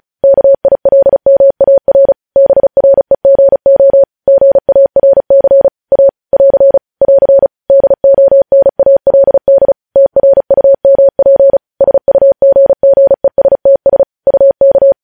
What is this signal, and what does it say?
Morse code, heard as a keyed tone, about 36 wpm, 550 Hz. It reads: KILMAR BREGO GARC A <AR> <AR> DONALD TRUMP SUGGESTS UK
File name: podcast / News headlines-1x-35x35